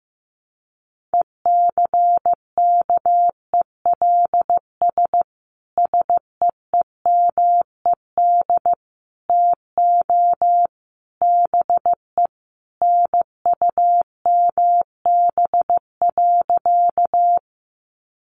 Morse code, heard as a keyed tone, about 15 wpm, 700 Hz. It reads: ECKELS SEEMED TO BE NUMB.
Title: A Sound Of Thunder 15wpm